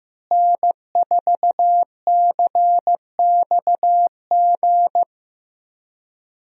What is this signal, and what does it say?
Morse code, heard as a keyed tone, about 16 wpm, 700 Hz. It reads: N4CXG
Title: USA Call Signs - 5 Characters - Repeated 3x - 15wpm